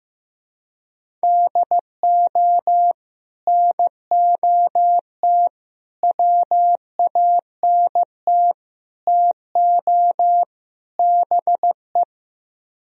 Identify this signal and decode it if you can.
Morse code, heard as a keyed tone, about 15 wpm, 700 Hz. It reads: DO NOT WANT TO BE
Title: Speed-Racing - Sets of 5 Words 15wpm